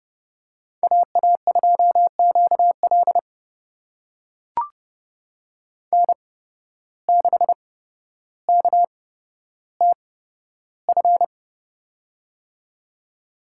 Morse code, heard as a keyed tone, about 30 wpm, spaced as for 13 wpm, 700 Hz. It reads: AA2QL N6KTF
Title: USA Call Signs - 5 Characters - 8x Character Spacing - 30wpm